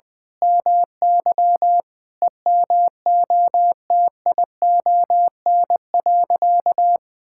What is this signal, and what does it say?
Morse code, heard as a keyed tone, about 20 wpm, 700 Hz. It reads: MY EMOTION.
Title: Sentences from Top 1600 Words 20wpm